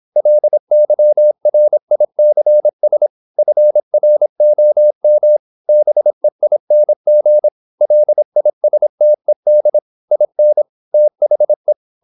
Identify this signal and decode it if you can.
Morse code, heard as a keyed tone, about 26 wpm, 600 Hz. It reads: LYRICS FROM BEING LISTED IN THE
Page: Never 26wpm | CW med Gnister